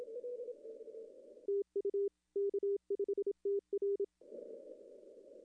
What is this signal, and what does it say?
Morse code, heard as a keyed tone, about 27 wpm, 410 Hz. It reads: TU K5TR
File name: contests / arrl-160-k5tr-op-w5tn -0400